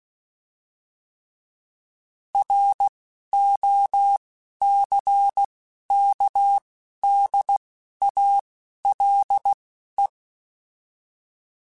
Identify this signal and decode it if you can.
Morse code, heard as a keyed tone, about 16 wpm, spaced as for 11 wpm, 800 Hz. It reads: ROCKDALE